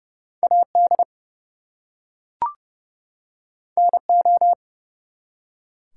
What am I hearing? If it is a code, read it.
Morse code, heard as a keyed tone, about 30 wpm, 700 Hz. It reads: AD NO
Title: Two Letter Words - 30wpm